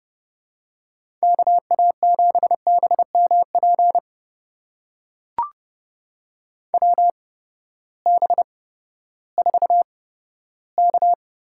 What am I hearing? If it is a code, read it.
Morse code, heard as a keyed tone, about 30 wpm, spaced as for 15 wpm, 700 Hz. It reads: KA7BMP WB4K
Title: USA Call Signs - 6 Characters - 8x Character Spacing - 30wpm